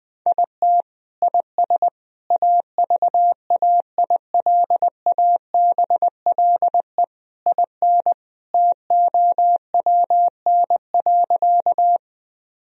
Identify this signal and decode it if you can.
Morse code, heard as a keyed tone, about 20 wpm, 700 Hz. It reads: IT IS AVAILABLE IN TOWN.
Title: Sentences from Top 1000 Words - Review Part 3 - 20wpm